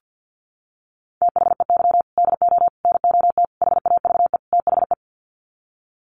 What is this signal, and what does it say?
Morse code, heard as a keyed tone, about 50 wpm, 700 Hz. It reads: THEY DO NOT HAVE THE
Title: Sets of 5 Words 50wpm